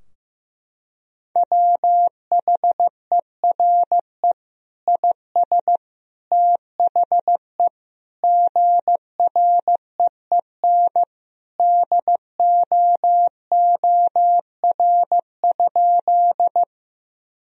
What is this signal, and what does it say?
Morse code, heard as a keyed tone, about 15 wpm, 700 Hz. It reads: WHERE IS THE GREEN DOOR?
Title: Sentences from Top 1000 Words - Review Part 2 - 15wpm